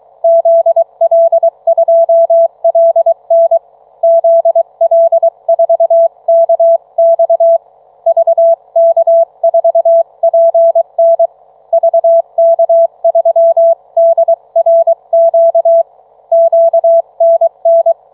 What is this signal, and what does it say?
Morse code, heard as a keyed tone, about 23 wpm, 660 Hz. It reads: ZL2LN ZL4KX VK4PN VK3DRQ QNN